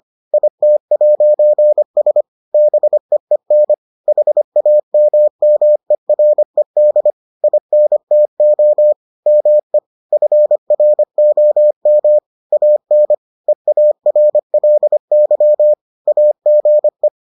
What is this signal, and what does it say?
Morse code, heard as a keyed tone, about 25 wpm, 600 Hz. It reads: IT'S BEEN HAMMERED INTO ME FROM AN EARLY AGE